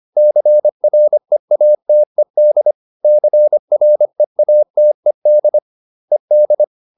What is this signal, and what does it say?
Morse code, heard as a keyed tone, about 25 wpm, 600 Hz. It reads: CREATED CREATED ED